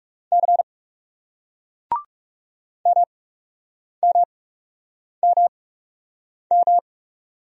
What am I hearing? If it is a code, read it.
Morse code, heard as a keyed tone, about 37 wpm, spaced as for 8 wpm, 700 Hz. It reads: C MMMM
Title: Speed-Racing Single Letters 30wpm